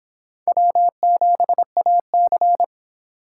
Morse code, heard as a keyed tone, about 25 wpm, 700 Hz. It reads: W7AC